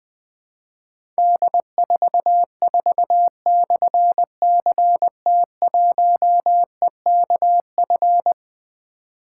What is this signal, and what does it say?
Morse code, heard as a keyed tone, about 20 wpm, 700 Hz. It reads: D44/CT1EKF
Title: Difficult Call Signs 20wpm